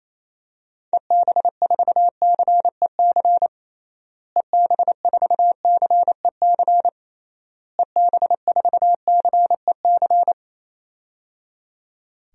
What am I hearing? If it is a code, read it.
Morse code, heard as a keyed tone, about 27 wpm, 700 Hz. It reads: EB4CEC EB4CEC EB4CEC
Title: DXCC-100 Call Signs - 6 Characters - Repeated 3x - 28wpm